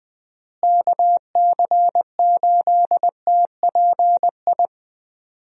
Morse code, heard as a keyed tone, about 21 wpm, 700 Hz. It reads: KC8TPI
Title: USA Call Signs - 6 Characters - Repeated 2x - 20wpm